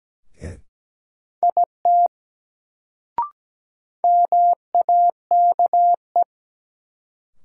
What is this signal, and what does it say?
Morse code, heard as a keyed tone, about 17 wpm, 700 Hz. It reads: IT MAKE
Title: Sentences from Top 100 Words - New Word Drill - 17wpm